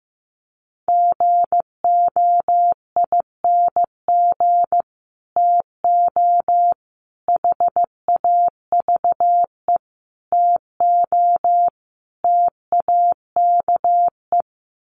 Morse code, heard as a keyed tone, about 15 wpm, 700 Hz. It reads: GOING TO HAVE TO TAKE
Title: Sets of 5 Words Encore 15wpm